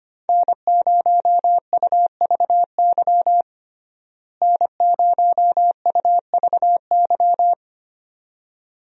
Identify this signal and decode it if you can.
Morse code, heard as a keyed tone, about 25 wpm, 700 Hz. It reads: N0UVY N0UVY
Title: USA Call Signs - 5 Characters - Repeated 2x - 25wpm